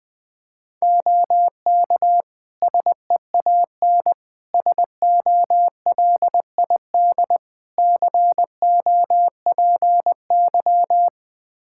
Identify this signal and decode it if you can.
Morse code, heard as a keyed tone, about 21 wpm, 700 Hz. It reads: OK SEAN SOLID COPY
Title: Practice QSO - Exchange 3 - More Info - 20wpm